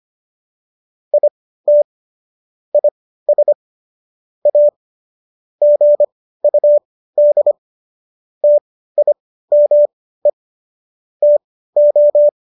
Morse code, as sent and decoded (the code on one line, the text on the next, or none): .. - / .. ... / .- / --. ..- -.. / - .. -- . / - ---
IT IS A GUD TIME TO